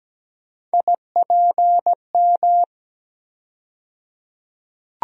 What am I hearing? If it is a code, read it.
Morse code, heard as a keyed tone, about 17 wpm, 700 Hz. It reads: IPM
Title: USA Call Sign Postfixes - 3 Letters 17wpm